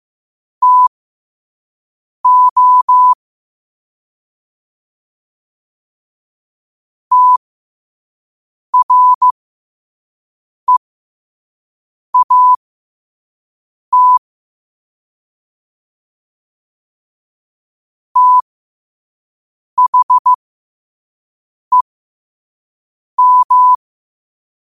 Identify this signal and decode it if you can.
Morse code, heard as a keyed tone, about 15 wpm, spaced as for 5 wpm, 1000 Hz. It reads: TO TREAT THEM